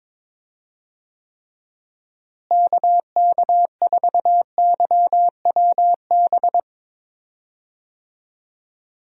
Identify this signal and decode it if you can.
Morse code, heard as a keyed tone, about 21 wpm, 700 Hz. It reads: KK4YWB
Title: USA Call Signs - 6 Characters - Repeated 3x - 22wpm